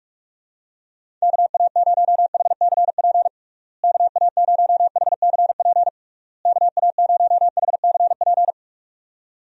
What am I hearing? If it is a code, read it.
Morse code, heard as a keyed tone, about 42 wpm, 700 Hz. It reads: KA0HCP KA0HCP KA0HCP
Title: USA Call Signs - 6 Characters - Repeated 3x - 45wpm